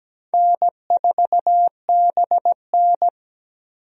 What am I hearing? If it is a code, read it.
Morse code, heard as a keyed tone, about 17 wpm, 700 Hz. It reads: N4BN
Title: USA Call Signs - 4 Characters - 17wpm